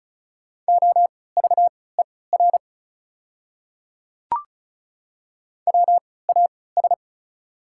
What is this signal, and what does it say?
Morse code, heard as a keyed tone, about 36 wpm, spaced as for 20 wpm, 690 Hz. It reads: OVER WAS
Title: 3x Character Spacing - Top 100 Words - 35wpm